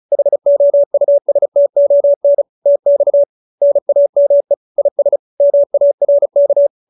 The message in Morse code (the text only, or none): HOUSTON TX NAME IS MARK